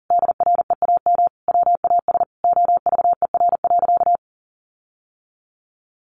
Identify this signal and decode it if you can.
Morse code, heard as a keyed tone, about 40 wpm, 700 Hz. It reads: DREAM WAS OVER.
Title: Sentences from Top 1500 Words 40wpm